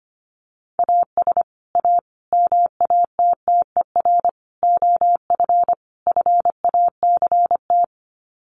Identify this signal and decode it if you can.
Morse code, heard as a keyed tone, about 25 wpm, 700 Hz. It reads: AS A MATTER OF FACT
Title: Sets of 5 Words 25wpm